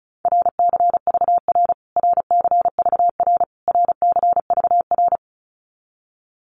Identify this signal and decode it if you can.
Morse code, heard as a keyed tone, about 34 wpm, 700 Hz. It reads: RCVR RCVR RCVR
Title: Abbreviations Repeated 3x - 35wpm